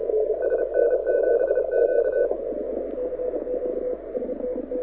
Morse code, heard as a keyed tone, about 31 wpm, 1500 Hz, with radio noise. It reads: SN7Q
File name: audio zawody / IARU 2015 CW SN7Q